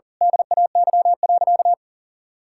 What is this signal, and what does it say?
Morse code, heard as a keyed tone, about 40 wpm, 700 Hz. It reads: DAY.
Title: Sentences from Top 1000 Words - Review Part 6 - 40wpm